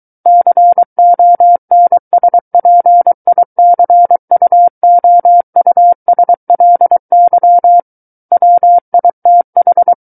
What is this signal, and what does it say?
Morse code, heard as a keyed tone, about 23 wpm, 700 Hz. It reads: CONSPICUOUSLY WITH
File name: Book One / 14-Chapter 14 - 23 WPM